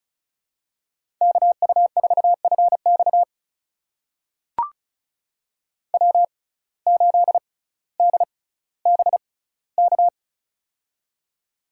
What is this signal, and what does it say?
Morse code, heard as a keyed tone, about 36 wpm, spaced as for 18 wpm, 700 Hz. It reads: KU4FX W8DBK
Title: USA Call Signs - 5 Characters - 6x Character Spacing - 35wpm